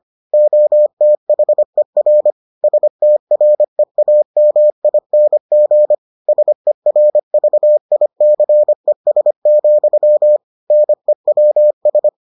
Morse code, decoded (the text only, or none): OTHER STREAMING SERVICES, NEWS